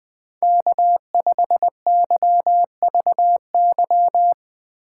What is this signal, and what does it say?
Morse code, heard as a keyed tone, about 20 wpm, 700 Hz. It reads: K5YVY
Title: K1USN SST QSO Practice - 20wpm